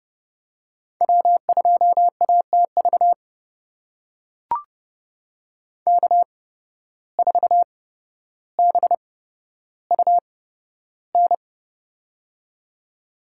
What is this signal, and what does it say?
Morse code, heard as a keyed tone, about 30 wpm, spaced as for 13 wpm, 700 Hz. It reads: W2ATV K4BUN
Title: USA Call Signs - 5 Characters - 8x Character Spacing - 30wpm